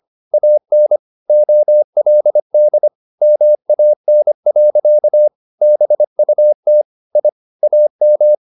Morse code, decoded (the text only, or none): AN OLD MAN. BUT I AM